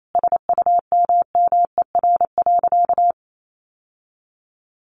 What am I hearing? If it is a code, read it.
Morse code, heard as a keyed tone, about 28 wpm, 700 Hz. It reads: SUMMER.
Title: Sentences from Top 1100 Words 28wpm